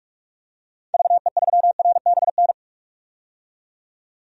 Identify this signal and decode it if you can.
Morse code, heard as a keyed tone, about 46 wpm, 690 Hz. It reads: VE3RBN
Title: DXCC-100 Call Signs - 6 Characters - 45wpm